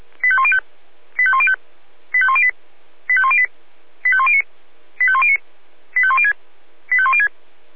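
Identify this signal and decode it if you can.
Five-tone selective call (ZVEI), heard as quick runs of five tones, tones of about 70 ms. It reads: 85105 85105 85108 85108 85109 85109 85106 85106
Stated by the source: ZVEI five-tone selective call received on an SDR.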